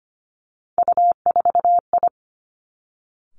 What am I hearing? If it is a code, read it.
Morse code, heard as a keyed tone, about 25 wpm, 700 Hz. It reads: U4I
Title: Three Challenging Letter-Number Combinations 25wpm